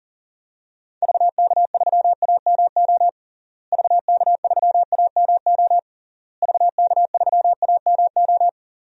Morse code, heard as a keyed tone, about 40 wpm, 700 Hz. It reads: VK3AMO VK3AMO VK3AMO
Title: DXCC-100 Call Signs - 6 Characters - Repeated 3x - 40wpm